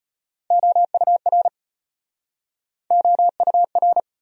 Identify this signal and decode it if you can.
Morse code, heard as a keyed tone, about 38 wpm, 700 Hz. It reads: OUR OUR